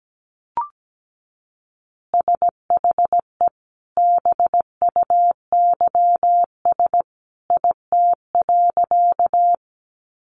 Morse code, heard as a keyed tone, about 17 wpm, 700 Hz. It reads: SHE BUYS IT.
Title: Additional Easy Phrases 17wpm